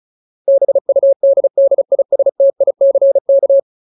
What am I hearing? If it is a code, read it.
Morse code, heard as a keyed tone, about 37 wpm, 550 Hz. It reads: BUDDISTICK
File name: Ultimate Word List / 35WPM Ultimate